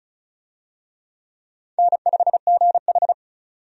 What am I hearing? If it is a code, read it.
Morse code, heard as a keyed tone, about 36 wpm, 700 Hz. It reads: N5GH
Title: USA Call Signs - 4 Characters - 35wpm